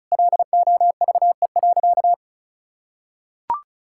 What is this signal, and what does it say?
Morse code, heard as a keyed tone, about 34 wpm, 700 Hz. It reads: LOVE.
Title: Sentences from Top 1000 Words - Review Part 6 - 35wpm